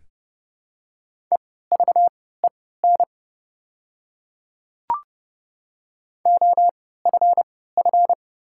Morse code, heard as a keyed tone, about 30 wpm, spaced as for 17 wpm, 690 Hz. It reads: EVEN OFF
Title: 3x Character Spacing - Top 200 Words - 30wpm